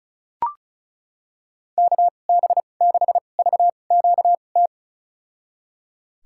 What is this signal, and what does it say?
Morse code, heard as a keyed tone, about 34 wpm, spaced as for 25 wpm, 700 Hz. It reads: KB6VQT
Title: USA Call Signs - 6 Characters - 2x Character Spacing - 35wpm